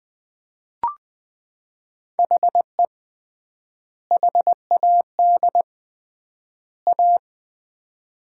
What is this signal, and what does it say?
Morse code, heard as a keyed tone, about 20 wpm, 700 Hz. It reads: HE HAD A
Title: 3x Word-Spacing - Sets of 3 Words - 20wpm